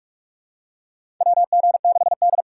Morse code, heard as a keyed tone, about 46 wpm, 700 Hz. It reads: WG6D
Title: USA Call Signs - 2x Character Spacing - 45wpm